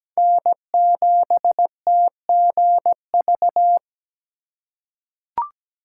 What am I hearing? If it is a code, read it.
Morse code, heard as a keyed tone, about 17 wpm, 700 Hz. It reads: N7TGV
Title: USA Call Signs - 5 Characters - 4x Character Spacing - 17wpm